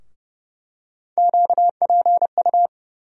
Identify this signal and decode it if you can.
Morse code, heard as a keyed tone, about 29 wpm, 700 Hz. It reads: QPU
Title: USA Call Sign Postfixes - 3 Letters 30wpm